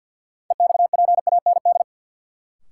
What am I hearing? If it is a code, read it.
Morse code, heard as a keyed tone, about 52 wpm, 700 Hz. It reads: EXPAND
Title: Sentences from Top 1400 Words - New Word Drill Alt - 50wpm